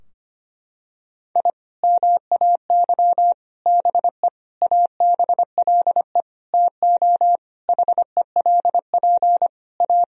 Morse code, heard as a keyed tone, about 25 wpm, 700 Hz. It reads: I MAY BE ABLE TO HELP A
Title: Sentences from Top 1000 Words - Review Part 2 - 25wpm